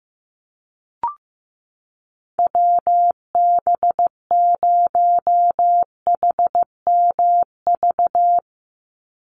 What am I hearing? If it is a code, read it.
Morse code, heard as a keyed tone, about 15 wpm, 700 Hz. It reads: WB0HMV
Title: USA Call Signs 15wpm